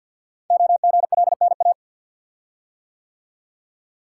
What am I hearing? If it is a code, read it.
Morse code, heard as a keyed tone, about 51 wpm, 700 Hz. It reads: OGLNA
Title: Code Groups - 5 Letters - 50wpm